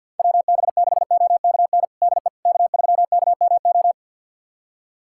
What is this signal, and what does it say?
Morse code, heard as a keyed tone, about 52 wpm, 700 Hz. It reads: WB6OKN DE K3KMO
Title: USA Callsign DE Callsign 50wpm